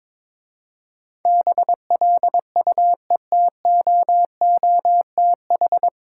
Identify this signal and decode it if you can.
Morse code, heard as a keyed tone, about 22 wpm, 700 Hz. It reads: BLUETOOTH
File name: Difficult Words with Hints - 22wpm